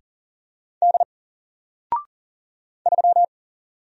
Morse code, heard as a keyed tone, about 40 wpm, 700 Hz. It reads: D 3
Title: Single Letter-Number Alt 40wpm